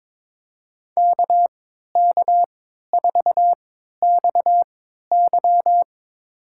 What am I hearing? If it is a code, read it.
Morse code, heard as a keyed tone, about 22 wpm, spaced as for 13 wpm, 700 Hz. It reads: KK4XY
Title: USA Call Signs - 3x Character Spacing - 22wpm